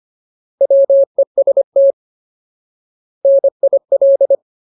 Morse code, heard as a keyed tone, about 25 wpm, 550 Hz. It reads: WEST NIL
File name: Ultimate Word List / 25WPM Ultimate